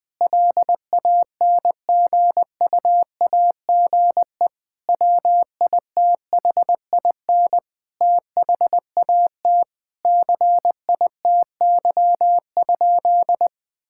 Morse code, heard as a keyed tone, about 20 wpm, 700 Hz. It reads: LANGUAGE WITHIN THAT CITY?
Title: Sentences from Top 1000 Words - Review Part 5 - 20wpm